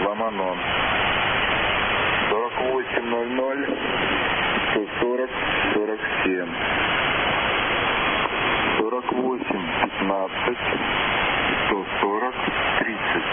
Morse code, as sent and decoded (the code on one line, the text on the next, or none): none